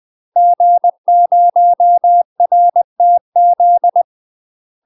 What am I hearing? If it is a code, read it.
Morse code, heard as a keyed tone, about 20 wpm, 700 Hz. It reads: G0RTZ
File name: UK MASTER QSOs TRAINING / 20WPM / UK MASTER QSOs 0043